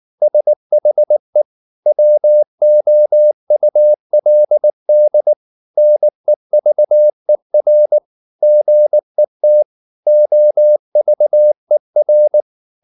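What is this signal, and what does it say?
Morse code, heard as a keyed tone, about 19 wpm, 600 Hz. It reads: SHE WOULD NEVER GET OVER
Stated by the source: Morse code at 19 wpm from CW med Gnister, reading Ken Follett's novel Never in English.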